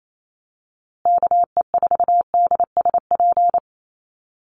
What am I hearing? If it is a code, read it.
Morse code, heard as a keyed tone, about 28 wpm, 700 Hz. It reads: KE4DSP
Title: USA Call Signs Repeated 3x 28wpm